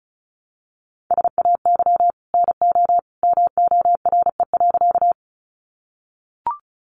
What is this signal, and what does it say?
Morse code, heard as a keyed tone, about 34 wpm, 700 Hz. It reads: SAY NO MORE.
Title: Sentences from Top 200 Words 35wpm